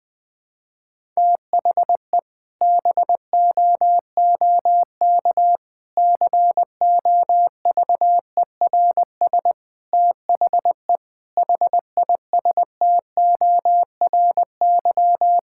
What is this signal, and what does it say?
Morse code, heard as a keyed tone, about 21 wpm, 700 Hz. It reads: THE BOOK COVERS THE HISTORY
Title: Sentences with 4-Word Phrases - 20wpm